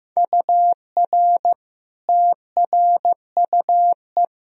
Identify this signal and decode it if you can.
Morse code, heard as a keyed tone, about 15 wpm, 700 Hz. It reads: UR TRUE
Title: Sentences from Top 400 Words and Abbreviations 15wpm